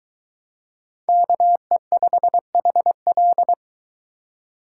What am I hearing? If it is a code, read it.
Morse code, heard as a keyed tone, about 22 wpm, 700 Hz. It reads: KE5HL